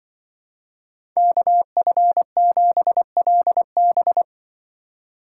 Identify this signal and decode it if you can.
Morse code, heard as a keyed tone, about 23 wpm, 700 Hz. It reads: KF7LB
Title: Speed Racing - USA 2x2 Call Signs 20wpm